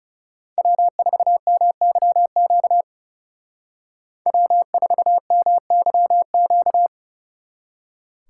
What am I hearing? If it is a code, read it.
Morse code, heard as a keyed tone, about 37 wpm, 700 Hz. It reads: W4MYQ W4MYQ
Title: Speed Racing - USA 1x3 Call Signs 30wpm